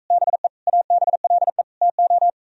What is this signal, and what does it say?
Morse code, heard as a keyed tone, about 44 wpm, 700 Hz. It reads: BE ABLE TO